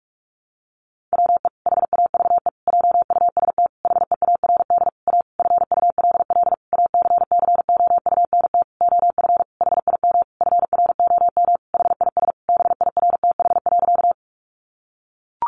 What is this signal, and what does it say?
Morse code, heard as a keyed tone, about 44 wpm, 700 Hz. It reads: WE HAVE JUST HEARD A FULL ACCOUNT OF HIM FROM HIS BIRTH.
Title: Sentences from Top 900 Words 45wpm